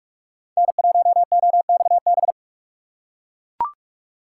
Morse code, heard as a keyed tone, about 44 wpm, 700 Hz. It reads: N1OXB
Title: USA Call Signs - 5 Characters - 6x Character Spacing - 45wpm